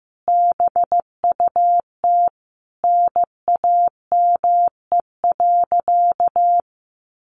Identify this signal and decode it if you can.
Morse code, heard as a keyed tone, about 15 wpm, 700 Hz. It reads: BUT NAME.
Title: Sentences from Top 1500 Words 15wpm